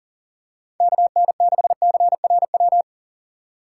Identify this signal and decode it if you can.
Morse code, heard as a keyed tone, about 40 wpm, 700 Hz. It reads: KN6CRW
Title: USA Call Signs - 6 Characters - 2x Character Spacing - 40wpm